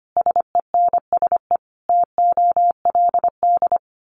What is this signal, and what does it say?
Morse code, heard as a keyed tone, about 25 wpm, 700 Hz. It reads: SENSE TOLD
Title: Sentences from Top 1200 Words 25wpm